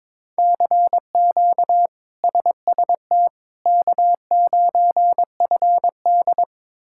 Morse code, heard as a keyed tone, about 22 wpm, 700 Hz. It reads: CQ SST K9FD
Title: K1USN SST QSO Practice - 22wpm